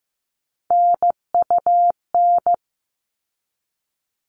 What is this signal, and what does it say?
Morse code, heard as a keyed tone, about 15 wpm, 700 Hz. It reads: NUN
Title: All Cut Numbers - 3-Digit 15wpm